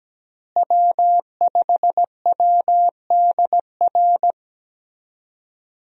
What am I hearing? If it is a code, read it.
Morse code, heard as a keyed tone, about 17 wpm, 700 Hz. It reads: W5WDR
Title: USA Call Signs - 5 Characters - Repeated 2x - 17wpm